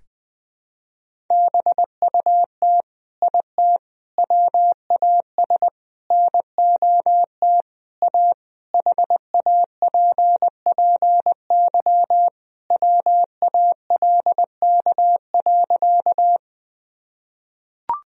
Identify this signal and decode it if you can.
Morse code, heard as a keyed tone, about 21 wpm, 700 Hz. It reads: BUT IT WAS NOT A HAPPY WALK.
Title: Sentences from Top 1000 Words - Review Part 1 - 20wpm